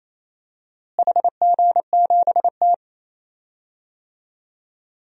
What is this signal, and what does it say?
Morse code, heard as a keyed tone, about 28 wpm, 700 Hz. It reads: HG7T
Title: DXCC-100 Call Signs - 4 Characters - Repeated 2x - 28wpm